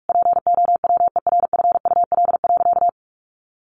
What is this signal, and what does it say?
Morse code, heard as a keyed tone, about 44 wpm, 700 Hz. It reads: POWERFUL.
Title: Sentences from Top 1300 Words 45wpm